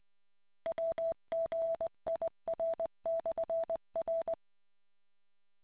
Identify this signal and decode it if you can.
Morse code, heard as a keyed tone, about 20 wpm, 660 Hz. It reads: WGIR/R